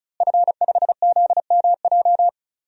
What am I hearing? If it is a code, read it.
Morse code, heard as a keyed tone, about 34 wpm, 700 Hz. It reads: F5ZMJ
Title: DXCC-100 Call Signs - 5 Characters - Repeated 2x - 35wpm